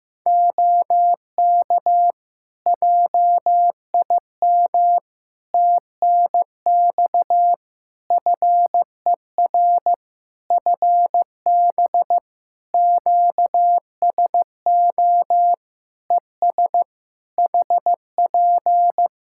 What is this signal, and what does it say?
Morse code, heard as a keyed tone, about 15 wpm, 700 Hz. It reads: OK JIM TNX FER FB QSO ES HP
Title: Practice QSO - Exchange 4 - Ending - 15wpm